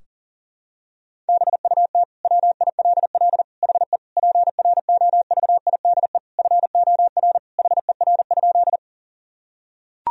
Morse code, heard as a keyed tone, about 40 wpm, 700 Hz. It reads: BUT WILL HE PROVIDE FOR HER?
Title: Sentences from Top 1000 Words - Review Part 3 - 40wpm